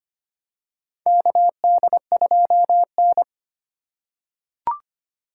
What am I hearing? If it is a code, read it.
Morse code, heard as a keyed tone, about 25 wpm, 700 Hz. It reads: KD2N
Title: USA Call Signs - 4x Character Spacing - 25wpm